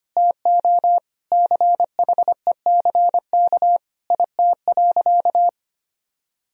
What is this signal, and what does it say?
Morse code, heard as a keyed tone, about 25 wpm, 700 Hz. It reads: TO CHECK IT.
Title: Sentences from Top 1000 Words - Review Part 3 - 25wpm